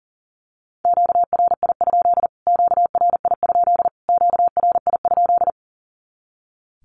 Morse code, heard as a keyed tone, about 40 wpm, 700 Hz. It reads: QRI? QRI? QRI?
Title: Q-Codes Repeated 3x 40wpm